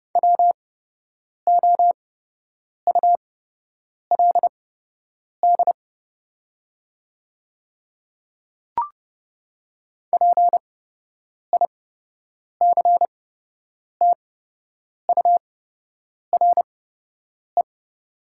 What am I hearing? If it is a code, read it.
Morse code, heard as a keyed tone, about 30 wpm, spaced as for 8 wpm, 700 Hz. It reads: WOULD PICTURE
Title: 8x Character Spacing - Top 300 Words - 30wpm